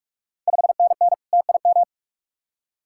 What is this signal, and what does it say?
Morse code, heard as a keyed tone, about 45 wpm, 700 Hz. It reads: 5NN TIM
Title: Ontario QSO Party Exchange 45wpm